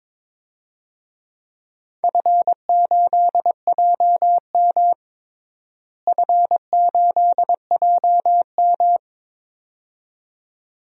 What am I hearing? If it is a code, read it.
Morse code, heard as a keyed tone, about 23 wpm, 700 Hz. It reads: F8JM F8JM
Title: DXCC-100 Call Signs - 4 Characters - Repeated 2x - 22wpm